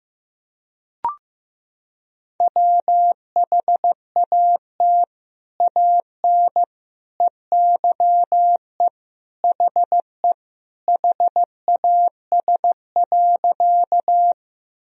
Morse code, heard as a keyed tone, about 15 wpm, 700 Hz. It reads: WHAT AN EYE HE HAS.
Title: Sentences from Top 1000 Words - Review Part 1 - 15wpm